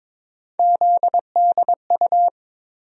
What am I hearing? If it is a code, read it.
Morse code, heard as a keyed tone, about 22 wpm, 700 Hz. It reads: ZDU
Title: USA Call Sign Postfixes - 3 Letters 22wpm